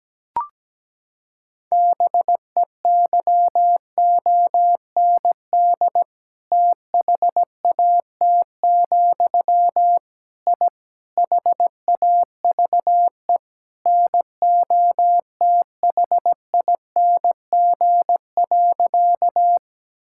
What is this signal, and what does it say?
Morse code, heard as a keyed tone, about 17 wpm, 700 Hz. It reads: BEYOND THAT, I HAVE NOTHING.
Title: Sentences from Top 1000 Words - Review Part 6 - 17wpm